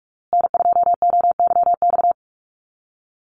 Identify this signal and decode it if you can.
Morse code, heard as a keyed tone, about 44 wpm, 690 Hz. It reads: N2OYX
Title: USA Call Signs Repeated 3x 45wpm